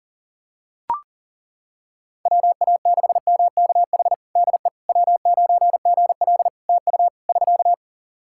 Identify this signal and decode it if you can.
Morse code, heard as a keyed tone, about 40 wpm, 700 Hz. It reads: WA6MKH DE W9GL TU <SK>
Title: Practice QSO - Exchange 4 - Ending - 40wpm